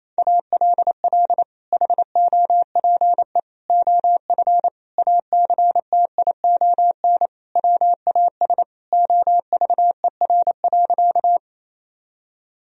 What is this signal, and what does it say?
Morse code, heard as a keyed tone, about 28 wpm, 700 Hz. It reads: ALL HOPE OF ACTION WAS OVER.
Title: Sentences from Top 1000 Words - Review Part 4 - 28wpm